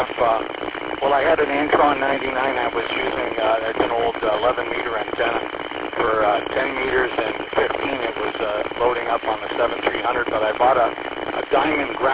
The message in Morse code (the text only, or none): none